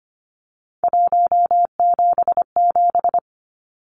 Morse code, heard as a keyed tone, about 25 wpm, 700 Hz. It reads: 177
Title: Numbers - 3-Digit 25wpm